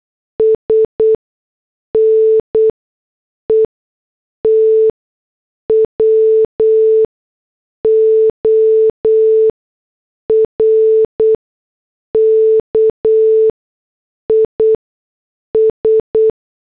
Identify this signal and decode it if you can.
Morse code, heard as a keyed tone, about 8 wpm, spaced as for 6 wpm, 440 Hz. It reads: S NETWORK IS
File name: morse / headlines